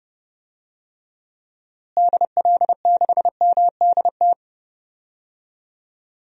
Morse code, heard as a keyed tone, about 30 wpm, 700 Hz. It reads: DL6MDT
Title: DXCC-100 Call Signs - 6 Characters - Repeated 2x - 30wpm